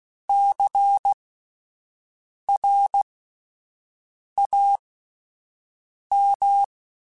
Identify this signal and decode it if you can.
Morse code, heard as a keyed tone, about 16 wpm, spaced as for 5 wpm, 800 Hz. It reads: CRAM
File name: Alphabetical / Alphabetical mp3s / Cs @5